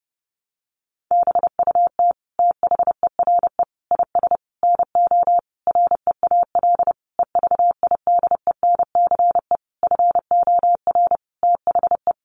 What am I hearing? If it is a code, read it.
Morse code, heard as a keyed tone, about 30 wpm, 700 Hz. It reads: BUT THERE IS NO REAL EVIDENCE FOR THE
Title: Sentences from Top 1300 Words 30wpm